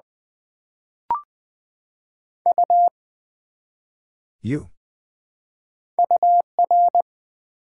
Morse code, heard as a keyed tone, about 20 wpm, 700 Hz. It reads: U UR
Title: Build A Word - 1 Letter At A Time - 5-Letter Words - 20wpm